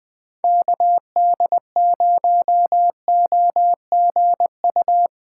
Morse code, heard as a keyed tone, about 21 wpm, 700 Hz. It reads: KD0OGU
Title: USA Callsign DE Callsign - 6 Characters - 20wpm